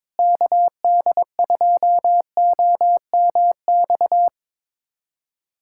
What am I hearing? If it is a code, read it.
Morse code, heard as a keyed tone, about 22 wpm, 700 Hz. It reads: KD2OMX
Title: USA Callsign DE Callsign 22wpm